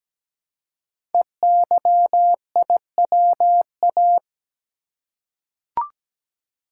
Morse code, heard as a keyed tone, about 17 wpm, 700 Hz. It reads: EYIWA
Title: Code Groups - 5 Letters - 17wpm